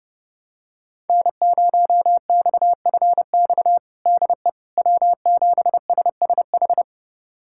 Morse code, heard as a keyed tone, about 30 wpm, 700 Hz. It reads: N0XFX DE W7SSH
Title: USA Callsign DE Callsign - 5 Characters - 30wpm